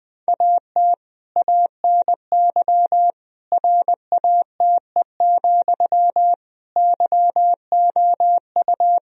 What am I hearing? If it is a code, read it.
Morse code, heard as a keyed tone, about 20 wpm, 700 Hz. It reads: AT ANY RATE, YOU
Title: Sentences from Top 1000 Words - Review Part 4 - 20wpm